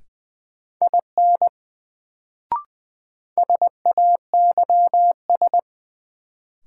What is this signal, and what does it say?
Morse code, heard as a keyed tone, about 20 wpm, 700 Hz. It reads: IN SAYS
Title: Sets of 2 Words - New Word Drill - 20wpm